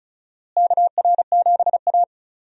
Morse code, heard as a keyed tone, about 36 wpm, 700 Hz. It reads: KR7A
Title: USA Call Signs - 4 Characters - Repeated 2x - 35wpm